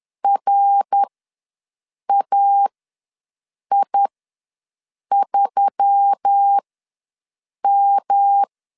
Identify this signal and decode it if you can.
Morse code, heard as a keyed tone, about 10 wpm, spaced as for 6 wpm, 800 Hz. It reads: RAI3M
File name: LEZIONI / 5 LBVC / 5 LB ws 10wpm 3616